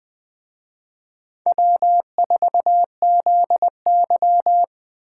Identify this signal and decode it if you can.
Morse code, heard as a keyed tone, about 20 wpm, 700 Hz. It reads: W4ZY